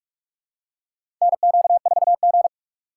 Morse code, heard as a keyed tone, about 46 wpm, 700 Hz. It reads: NQ4G
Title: USA Call Signs - 4x Character Spacing - 45wpm